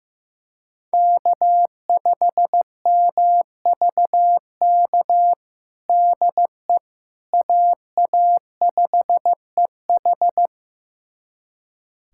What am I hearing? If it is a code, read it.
Morse code, heard as a keyed tone, about 16 wpm, 700 Hz. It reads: K5MVK DE AA5EH
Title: USA Callsign DE Callsign - 5 Characters - 15wpm